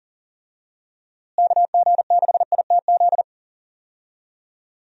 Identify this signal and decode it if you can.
Morse code, heard as a keyed tone, about 39 wpm, 700 Hz. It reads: KG6ITZ
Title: USA Call Signs - 6 Characters - Repeated 2x - 40wpm